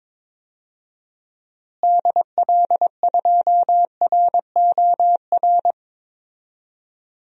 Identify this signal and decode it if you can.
Morse code, heard as a keyed tone, about 21 wpm, 700 Hz. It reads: DL2ROR
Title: DXCC-100 Call Signs - 6 Characters - Repeated 2x - 22wpm